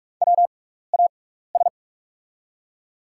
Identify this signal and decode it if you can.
Morse code, heard as a keyed tone, about 44 wpm, spaced as for 15 wpm, 700 Hz. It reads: WAS
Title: 6x Character Spacing - Top 300 Words - 45wpm